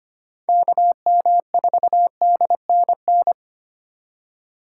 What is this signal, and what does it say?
Morse code, heard as a keyed tone, about 32 wpm, 700 Hz. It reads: KM4DNN